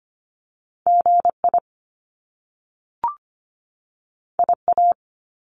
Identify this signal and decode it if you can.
Morse code, heard as a keyed tone, about 25 wpm, 700 Hz. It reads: GI IA
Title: Top Two Letter Combinations in Words 25wpm